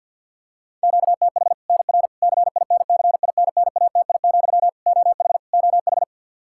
Morse code, heard as a keyed tone, about 52 wpm, 700 Hz. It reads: QTH NR CINCINNATI, OH OH
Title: Practice QSO - Exchange 1 - Beginning - 50wpm